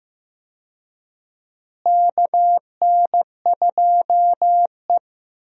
Morse code, heard as a keyed tone, about 15 wpm, 700 Hz. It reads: KN2E
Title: USA Call Signs - 4 Characters - Repeated 3x - 15wpm